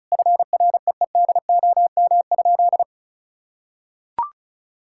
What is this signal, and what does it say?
Morse code, heard as a keyed tone, about 36 wpm, 700 Hz. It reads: FREEDOM?
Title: Sentences from Top 1000 Words - Review Part 5 - 35wpm